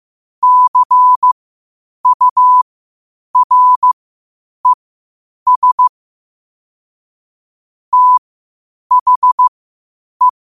Morse code, heard as a keyed tone, about 15 wpm, spaced as for 8 wpm, 1000 Hz. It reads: CURES THE
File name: qotdfiles / 8wpm